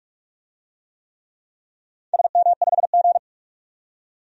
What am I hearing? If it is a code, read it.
Morse code, heard as a keyed tone, about 44 wpm, 700 Hz. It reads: SM5G
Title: DXCC-100 Call Signs - 4 Characters - 45wpm